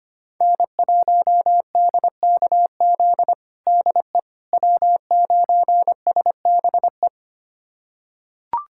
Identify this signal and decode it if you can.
Morse code, heard as a keyed tone, about 25 wpm, 700 Hz. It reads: N1DKZ DE W9SBE
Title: USA Callsign DE Callsign - 5 Characters - 25wpm